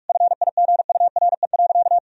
Morse code, heard as a keyed tone, about 46 wpm, 700 Hz. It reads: FIGURE.
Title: Sentences from Top 1000 Words - Review Part 4 - 45wpm